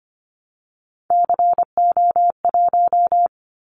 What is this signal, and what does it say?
Morse code, heard as a keyed tone, about 25 wpm, 700 Hz. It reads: CO1
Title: Callsign Prefixes for Top 100 DXCC - 25wpm